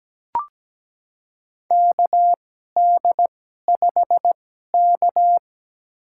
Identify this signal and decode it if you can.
Morse code, heard as a keyed tone, about 17 wpm, spaced as for 12 wpm, 700 Hz. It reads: KD5K
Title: USA Call Signs - 4 Characters - 2x Character Spacing - 17wpm